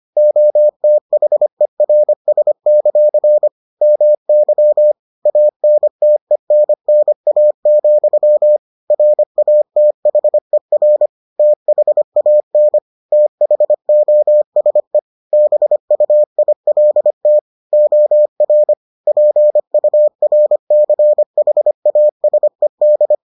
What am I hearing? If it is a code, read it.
Morse code, heard as a keyed tone, about 25 wpm, 600 Hz. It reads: OTHERS; MY ANTENNA, RATHER THAN THOSE BUILT OR PURCHASED